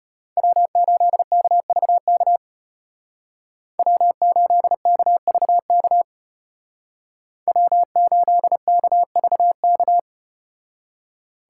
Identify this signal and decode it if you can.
Morse code, heard as a keyed tone, about 32 wpm, 700 Hz. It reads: W8KVK W8KVK W8KVK